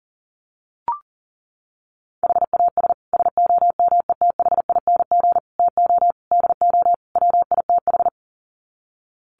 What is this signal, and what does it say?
Morse code, heard as a keyed tone, about 37 wpm, 700 Hz. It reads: HAS SOMETHING TO DO WITH
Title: Sets of 5 Words Encore 40wpm